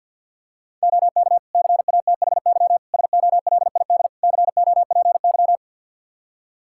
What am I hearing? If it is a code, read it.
Morse code, heard as a keyed tone, about 51 wpm, 700 Hz. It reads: OK CATHY SOLID COPY
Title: K1USN SST QSO Practice 50wpm